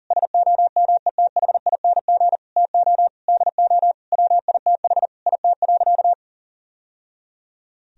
Morse code, heard as a keyed tone, about 40 wpm, 700 Hz. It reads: SOMETHING TO DO WITH IT.
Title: Sentences from Top 1000 Words - Review Part 2 - 40wpm